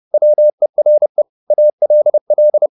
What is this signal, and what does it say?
Morse code, heard as a keyed tone, about 30 wpm, 600 Hz. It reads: WERE ALL
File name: the old man and the sea / the old man and the sea 1 30wpm